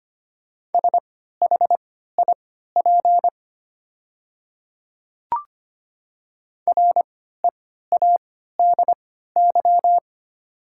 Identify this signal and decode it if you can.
Morse code, heard as a keyed tone, about 25 wpm, spaced as for 14 wpm, 700 Hz. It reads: SHIP READY
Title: 3x Character Spacing - Top 500 Words - 25wpm